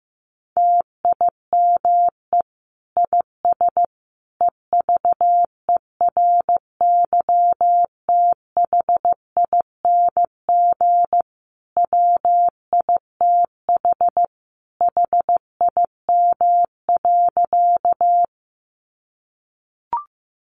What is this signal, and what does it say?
Morse code, heard as a keyed tone, about 15 wpm, 700 Hz. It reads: TIME IS EVERYTHING WITH HIM.
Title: Sentences from Top 700 Words 15wpm